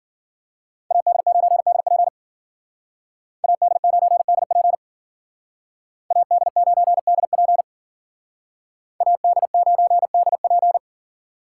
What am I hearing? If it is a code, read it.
Morse code, heard as a keyed tone, about 52 wpm, 700 Hz. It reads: AD9DP AD9DP AD9DP AD9DP